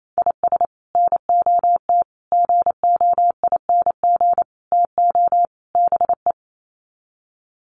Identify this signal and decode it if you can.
Morse code, heard as a keyed tone, about 27 wpm, 700 Hz. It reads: IS NOT GOING TO BE
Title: Sets of 5 Words 28wpm